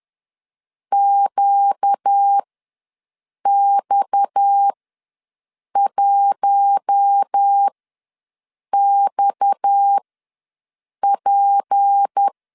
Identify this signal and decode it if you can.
Morse code, heard as a keyed tone, about 10 wpm, spaced as for 6 wpm, 800 Hz. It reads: QX1XP
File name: LEZIONI / 6 PKX0 / 6 PKX ws 10wpm 3706